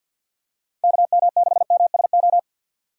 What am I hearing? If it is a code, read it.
Morse code, heard as a keyed tone, about 52 wpm, 700 Hz. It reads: KM6MSO
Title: USA Call Signs - 6 Characters - 3x Character Spacing - 50wpm